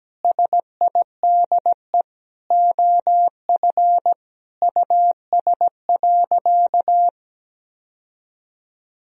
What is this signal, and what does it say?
Morse code, heard as a keyed tone, about 17 wpm, 700 Hz. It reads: SIDE OF US.
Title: Sentences from Top 1000 Words - Review Part 3 - 17wpm